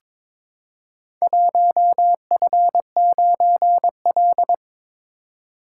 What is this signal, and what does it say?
Morse code, heard as a keyed tone, about 22 wpm, 700 Hz. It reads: 1F9L
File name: practice / 22wpm / Four Letter-Number Combinations - 22wpm